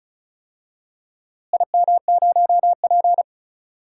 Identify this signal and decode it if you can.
Morse code, heard as a keyed tone, about 36 wpm, 700 Hz. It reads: IM0P
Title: DXCC-100 Call Signs - 4 Characters - Repeated 3x - 35wpm